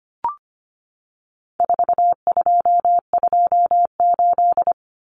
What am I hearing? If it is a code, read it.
Morse code, heard as a keyed tone, about 25 wpm, 700 Hz. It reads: 4228
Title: Numbers - 4-Digit 25wpm